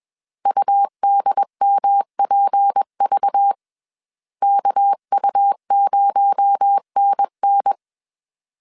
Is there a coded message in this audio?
Yes: Morse code from 0.5 to 7.7 s, heard as a keyed tone, about 21 wpm, 800 Hz. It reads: UDMPV KU0NN